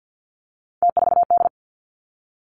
Morse code, heard as a keyed tone, about 50 wpm, 700 Hz. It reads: T4D
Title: Three Challenging Letter-Number Combinations 50wpm